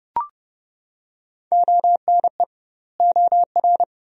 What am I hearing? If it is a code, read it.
Morse code, heard as a keyed tone, about 31 wpm, 700 Hz. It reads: ONE OR
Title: 2x Word-Spacing - Sentences from Top 100 Words - 30wpm